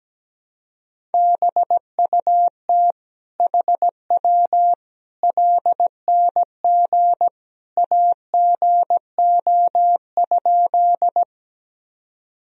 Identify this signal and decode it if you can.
Morse code, heard as a keyed tone, about 17 wpm, 700 Hz. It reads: BUT HW LNG AGO?
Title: Sentences from Top 500 Words and Abbreviations 17wpm